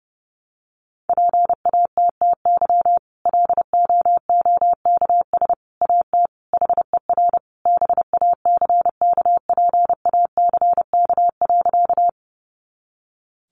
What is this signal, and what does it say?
Morse code, heard as a keyed tone, about 30 wpm, 700 Hz. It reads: PATTY LOOKS AT HER BACKPACK.
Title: Additional Easy Phrases 30wpm